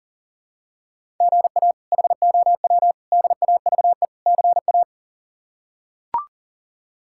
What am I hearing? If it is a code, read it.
Morse code, heard as a keyed tone, about 40 wpm, 700 Hz. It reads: GA HOW DAVE CA
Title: K1USN SST QSO Practice - 40wpm